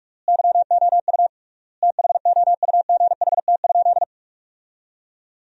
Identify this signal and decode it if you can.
Morse code, heard as a keyed tone, about 48 wpm, 700 Hz. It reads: YOU THOUGHT?
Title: 3x Word-Spacing - Sentences from Top 300 Words - 45wpm